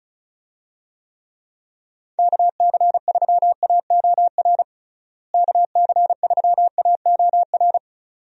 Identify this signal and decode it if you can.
Morse code, heard as a keyed tone, about 34 wpm, 700 Hz. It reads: KC3AOR KC3AOR
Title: USA Call Signs - 6 Characters - Repeated 2x - 35wpm